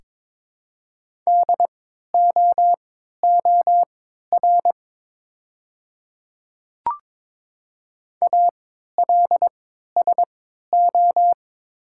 Morse code, heard as a keyed tone, about 22 wpm, spaced as for 12 wpm, 690 Hz. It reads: DOOR ALSO
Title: 3x Character Spacing - Top 300 Words - 22wpm